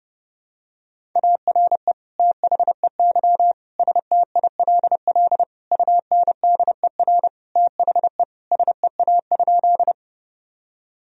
Morse code, heard as a keyed tone, about 32 wpm, 700 Hz. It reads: ARE THEY STILL UNDER THE SEA?
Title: Sentences from Top 1000 Words - Review Part 1 - 30wpm